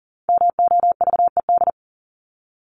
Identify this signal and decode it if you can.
Morse code, heard as a keyed tone, about 40 wpm, 700 Hz. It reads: MOVED
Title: ED Suffix - Top 100 Words with Base Words 40wpm